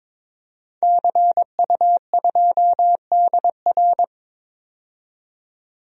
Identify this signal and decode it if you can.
Morse code, heard as a keyed tone, about 21 wpm, 700 Hz. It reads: CU2DR
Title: DXCC-100 Call Signs - 5 Characters - 22wpm